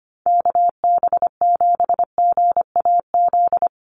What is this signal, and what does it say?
Morse code, heard as a keyed tone, about 25 wpm, 700 Hz. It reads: KB7GAZ
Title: USA Call Signs Repeated 2x 25wpm